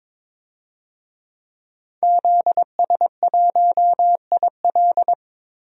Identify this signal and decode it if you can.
Morse code, heard as a keyed tone, about 22 wpm, 700 Hz. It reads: ZS1IL
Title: DXCC-100 Call Signs - 5 Characters - 22wpm